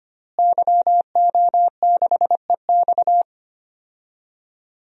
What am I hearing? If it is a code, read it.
Morse code, heard as a keyed tone, about 25 wpm, 700 Hz. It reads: YO6EX
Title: DXCC-100 Call Signs - 5 Characters - Repeated 3x - 25wpm